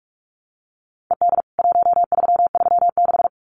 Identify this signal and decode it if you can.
Morse code, heard as a keyed tone, about 44 wpm, 700 Hz. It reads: ED 1336
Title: CWT Exchange 45wpm